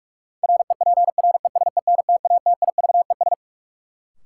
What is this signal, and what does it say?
Morse code, heard as a keyed tone, about 41 wpm, 700 Hz. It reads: REPRESENTATIVES
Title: Sets of 5 Words Encore - New Word Drill - 45wpm